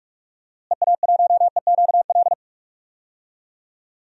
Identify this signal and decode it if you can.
Morse code, heard as a keyed tone, about 43 wpm, 700 Hz. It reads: EA1EQL
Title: DXCC-100 Call Signs - 6 Characters - Repeated 2x - 45wpm